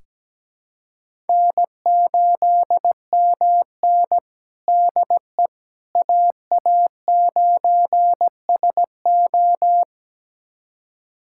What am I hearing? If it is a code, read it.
Morse code, heard as a keyed tone, about 17 wpm, 700 Hz. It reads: N8MN DE AA9SO
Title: USA Callsign DE Callsign 17wpm